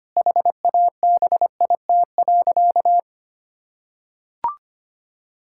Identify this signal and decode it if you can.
Morse code, heard as a keyed tone, about 25 wpm, 700 Hz. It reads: HABIT.
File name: practice / 25wpm / Difficult Sentences with Hints - 25wpm